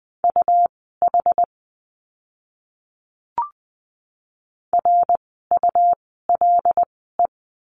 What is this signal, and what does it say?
Morse code, heard as a keyed tone, about 20 wpm, spaced as for 14 wpm, 700 Hz. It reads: UH RULE
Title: 101 Lesson 17f - Letter u - Words 20wpm